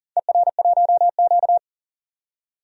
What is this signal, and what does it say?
Morse code, heard as a keyed tone, about 39 wpm, 700 Hz. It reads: ER1Q
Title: DXCC-100 Call Signs - 4 Characters - 40wpm